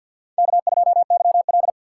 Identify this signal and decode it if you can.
Morse code, heard as a keyed tone, about 50 wpm, 700 Hz. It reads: K2YL
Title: USA Call Signs - 4 Characters - 6x Character Spacing - 50wpm